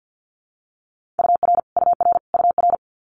Morse code, heard as a keyed tone, about 50 wpm, 700 Hz. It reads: UR UR UR
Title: Multi-letter Phonemes Repeated 3x 50wpm